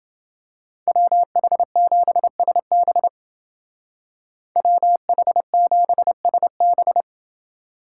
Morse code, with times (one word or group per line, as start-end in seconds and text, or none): 0.9-3.1 WH7SB
4.6-7.0 WH7SB